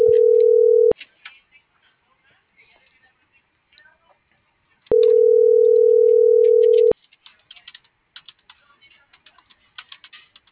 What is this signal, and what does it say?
Ringback tone (North America), 440 + 480 Hz.
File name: CENTRAL GENERAL / out ricardo.diaz - .57378